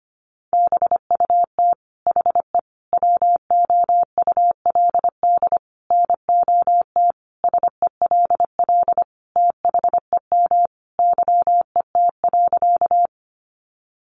Morse code, heard as a keyed tone, about 25 wpm, 700 Hz. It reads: BUT HE WOULD NOT SELL THEM YET.
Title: Sentences from Top 1100 Words 25wpm